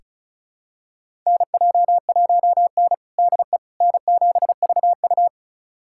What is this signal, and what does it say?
Morse code, heard as a keyed tone, about 34 wpm, 700 Hz. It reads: NJ1N DE N7VU
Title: USA Callsign DE Callsign - 4 Characters - 35wpm